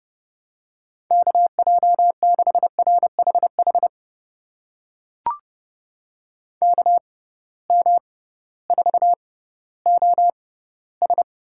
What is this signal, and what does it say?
Morse code, heard as a keyed tone, about 30 wpm, spaced as for 16 wpm, 700 Hz. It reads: KJ6RHH KM4OS
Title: USA Call Signs - 6 Characters - 6x Character Spacing - 30wpm